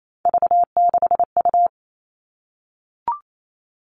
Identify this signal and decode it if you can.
Morse code, heard as a keyed tone, about 28 wpm, 700 Hz. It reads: V6U
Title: Three Challenging Letter-Number Combinations 28wpm